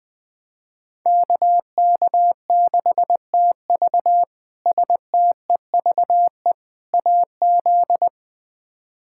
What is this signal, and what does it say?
Morse code, heard as a keyed tone, about 21 wpm, 700 Hz. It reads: KK6TV STEVE AZ
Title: K1USN SST QSO Practice - 20wpm